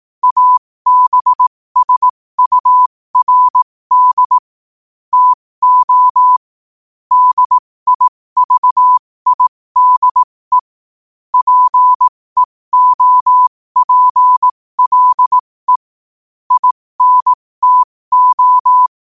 Morse code, heard as a keyed tone, about 18 wpm, spaced as for 15 wpm, 1000 Hz. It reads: ABSURD TO DIVIDE PEOPLE INTO
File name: qotdfiles / 15wpm